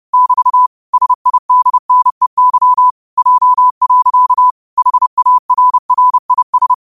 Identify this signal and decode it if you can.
Morse code, heard as a keyed tone, about 30 wpm, 1000 Hz. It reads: <BT> SIDNEY J. HARRIS
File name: qotdfiles / 30wpm